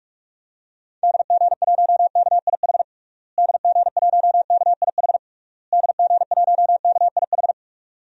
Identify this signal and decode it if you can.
Morse code, heard as a keyed tone, about 48 wpm, 700 Hz. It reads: DG1KIH DG1KIH DG1KIH
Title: DXCC-100 Call Signs - 6 Characters - Repeated 3x - 45wpm